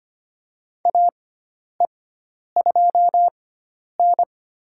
Morse code, heard as a keyed tone, about 25 wpm, spaced as for 10 wpm, 700 Hz. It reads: AE2N
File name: Lesson 16g - Letter c - Call Signs 25wpm